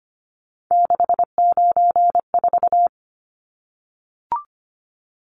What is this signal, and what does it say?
Morse code, heard as a keyed tone, about 25 wpm, 700 Hz. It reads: 694
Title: Numbers - 3-Digit 25wpm